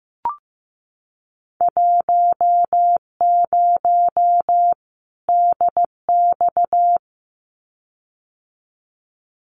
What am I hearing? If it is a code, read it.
Morse code, heard as a keyed tone, about 15 wpm, 700 Hz. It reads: 10 DX
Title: Winter Field Day Exchanges 15wpm